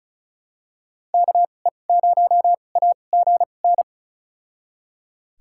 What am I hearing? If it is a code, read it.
Morse code, heard as a keyed tone, about 34 wpm, spaced as for 25 wpm, 700 Hz. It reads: KE0AGN
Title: USA Call Signs - 6 Characters - 2x Character Spacing - 35wpm